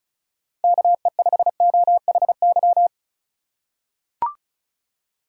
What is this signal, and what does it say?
Morse code, heard as a keyed tone, about 36 wpm, 700 Hz. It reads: KE5OHY
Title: USA Call Signs - 6 Characters - 2x Character Spacing - 35wpm